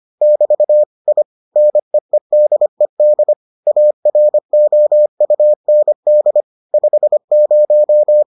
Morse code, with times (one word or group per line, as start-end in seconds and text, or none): 0.2-0.8 <BT>
1.1-1.2 I
1.6-3.3 NEEDED
3.7-6.4 AROUND
6.7-8.2 50